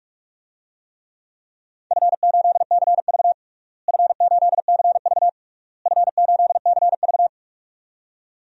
Morse code, heard as a keyed tone, about 41 wpm, 700 Hz. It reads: F8CV F8CV F8CV
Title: DXCC-100 Call Signs - 4 Characters - Repeated 3x - 45wpm